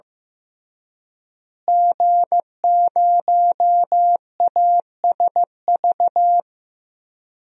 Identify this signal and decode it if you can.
Morse code, heard as a keyed tone, about 15 wpm, 700 Hz. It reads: G0ASV
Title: DXCC-100 Call Signs - 5 Characters - Repeated 3x - 15wpm